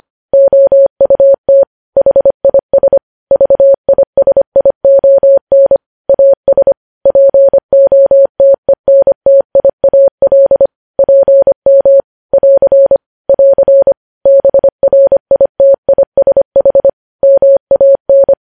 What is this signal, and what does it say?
Morse code, heard as a keyed tone, about 25 wpm, 550 Hz. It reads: OUT HIS VISION AS POTENTIAL PM <AR> <AR> BRITISH MAN
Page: news-headlines-in-morse-code-at-25-wpm